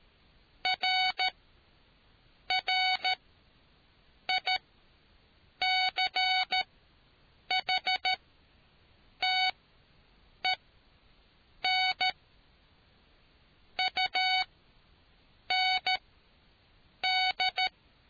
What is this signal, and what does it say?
Morse code, heard as a keyed tone, about 13 wpm, spaced as for 6 wpm, 750 Hz. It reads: RRICHTEN UND